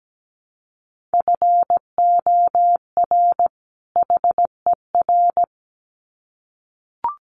Morse code, heard as a keyed tone, about 17 wpm, 700 Hz. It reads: FOR HER
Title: Sets of 2 Words 17wpm